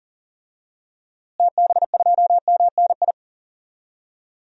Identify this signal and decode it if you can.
Morse code, heard as a keyed tone, about 40 wpm, 700 Hz. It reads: TB2MNI
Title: DXCC-100 Call Signs - 6 Characters - Repeated 2x - 40wpm